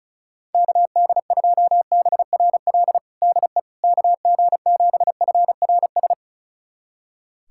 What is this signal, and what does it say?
Morse code, heard as a keyed tone, about 36 wpm, 700 Hz. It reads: KD2BRL DE KG7FRS
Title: USA Callsign DE Callsign 35wpm